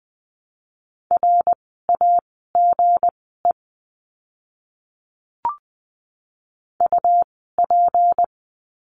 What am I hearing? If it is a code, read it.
Morse code, heard as a keyed tone, about 20 wpm, spaced as for 15 wpm, 700 Hz. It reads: RAGE UP
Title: 160 Lesson 26f - Letter g - Words 20wpm